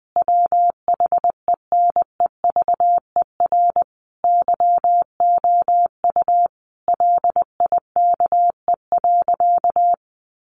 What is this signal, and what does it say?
Morse code, heard as a keyed tone, about 20 wpm, 700 Hz. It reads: WHENEVER YOU LIKE.
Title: Sentences from Top 1400 Words 20wpm